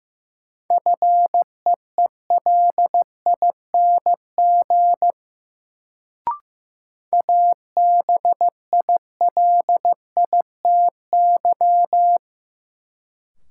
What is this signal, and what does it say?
Morse code, heard as a keyed tone, about 15 wpm, 700 Hz. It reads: FEELING ABILITY
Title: Sentences from Top 1200 Words - New Word Drill Alt - 15wpm